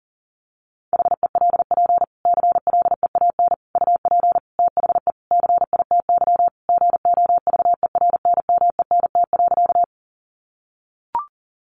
Morse code, heard as a keyed tone, about 40 wpm, 700 Hz. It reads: HELP CLEAN UP THE CITY GOVERNMENT.
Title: Sentences from Top 1100 Words 40wpm